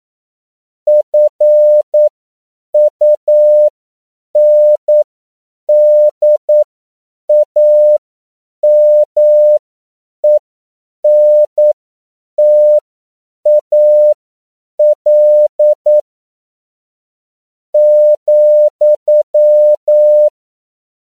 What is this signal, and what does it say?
Morse code, heard as a keyed tone, about 9 wpm, spaced as for 7 wpm, 600 Hz. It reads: FUNDAMENTAL ,
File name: CW / 10PPM